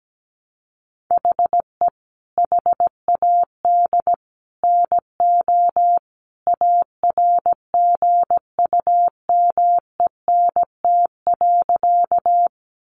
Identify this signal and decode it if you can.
Morse code, heard as a keyed tone, about 17 wpm, 700 Hz. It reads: HE HAD NO ARGUMENT.
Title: Sentences from Top 1400 Words 17wpm